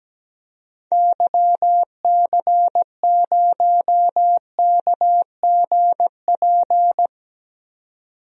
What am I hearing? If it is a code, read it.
Morse code, heard as a keyed tone, about 17 wpm, 700 Hz. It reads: YC0KGP
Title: DXCC-100 Call Signs - 6 Characters - 17wpm